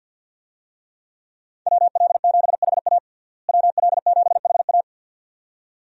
Morse code, heard as a keyed tone, about 52 wpm, 700 Hz. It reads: WL7HA WL7HA
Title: USA Call Signs - 5 Characters - Repeated 2x - 50wpm